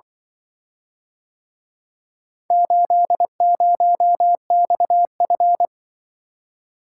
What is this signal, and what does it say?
Morse code, heard as a keyed tone, about 23 wpm, 700 Hz. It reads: 80XF
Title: Speed-Racing - Four Letter-Number Combinations - 20wpm